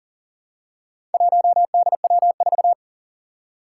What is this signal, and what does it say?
Morse code, heard as a keyed tone, about 37 wpm, 700 Hz. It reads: 1DW4
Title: Four Challenging Letter-Number Combinations 40wpm